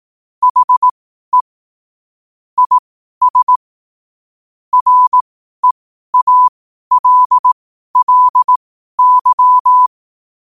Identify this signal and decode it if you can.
Morse code, heard as a keyed tone, about 18 wpm, spaced as for 12 wpm, 1000 Hz. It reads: HE IS REALLY
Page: Quotes for Fri, 15 Aug 2025 in Morse Code at 12 words per minute.